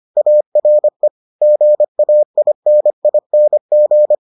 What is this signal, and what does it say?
Morse code, heard as a keyed tone, about 25 wpm, 600 Hz. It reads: ARE GAINING